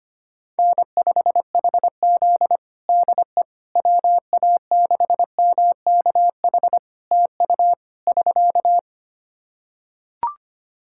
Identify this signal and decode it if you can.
Morse code, heard as a keyed tone, about 25 wpm, 700 Hz. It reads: N5HZ DE WA6MKH TU <SK>
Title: Practice QSO - Exchange 4 - Ending - 25wpm